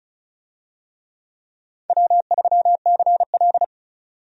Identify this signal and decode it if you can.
Morse code, heard as a keyed tone, about 34 wpm, 700 Hz. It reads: W3CL
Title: USA Call Signs - 4 Characters - 35wpm